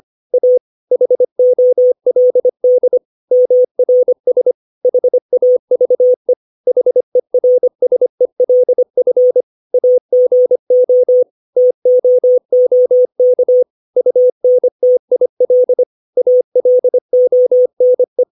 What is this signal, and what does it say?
Morse code, heard as a keyed tone, about 25 wpm, 500 Hz. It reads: A HOLD MRS HAVE HERSELF AGO TOOK UNTIL ALONE
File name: 632 words / 632 Words 25WPM